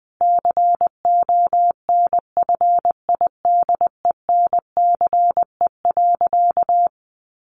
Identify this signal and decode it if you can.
Morse code, heard as a keyed tone, about 19 wpm, 700 Hz. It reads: CONFIDENCE.
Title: Sentences from Top 1500 Words 20wpm